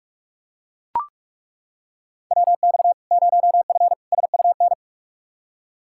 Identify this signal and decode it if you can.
Morse code, heard as a keyed tone, about 42 wpm, 700 Hz. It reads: WX 0F SUN
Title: Practice QSO - Exchange 2 - Info - 45wpm